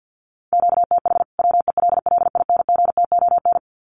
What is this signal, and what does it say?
Morse code, heard as a keyed tone, about 50 wpm, 700 Hz. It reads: QTH WELLINGTON
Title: Common QSO Phrases 50wpm